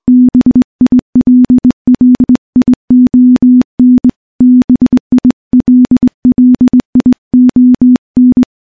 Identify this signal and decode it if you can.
Morse code, heard as a keyed tone, about 20 wpm, 250 Hz. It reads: BILLION BILLION
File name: morse / morse-training-20WPM